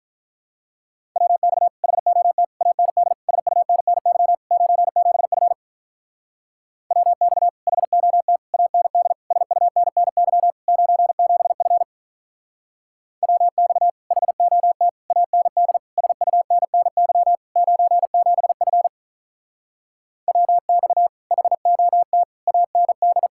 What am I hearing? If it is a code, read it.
Morse code, heard as a keyed tone, about 47 wpm, 700 Hz. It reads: WX HOT AND SUNNY 97F WX HOT AND SUNNY 97F WX HOT AND SUNNY 97F WX HOT AND